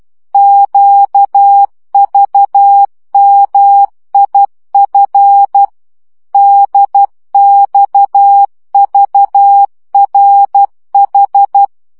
Morse code, heard as a keyed tone, about 12 wpm, 800 Hz. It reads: QVMIF DXVRH
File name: morze vizsga 2011-3 10